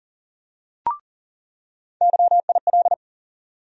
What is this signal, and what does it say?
Morse code, heard as a keyed tone, about 40 wpm, 700 Hz. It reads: YIL
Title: USA Call Sign Postfixes - 3 Letters 40wpm